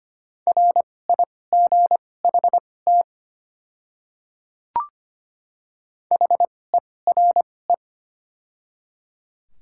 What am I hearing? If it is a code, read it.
Morse code, heard as a keyed tone, about 25 wpm, spaced as for 18 wpm, 700 Hz. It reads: RIGHT HERE
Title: 2x Character Spacing - Top 200 Words - 25wpm